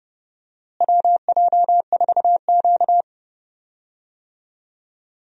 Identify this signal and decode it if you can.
Morse code, heard as a keyed tone, about 30 wpm, 700 Hz. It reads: WJ4Q
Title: USA Call Signs - 4 Characters - Repeated 3x - 30wpm